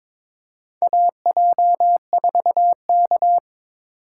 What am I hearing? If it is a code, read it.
Morse code, heard as a keyed tone, about 22 wpm, 700 Hz. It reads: AJ4K
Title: USA Call Signs - 4 Characters - 3x Character Spacing - 22wpm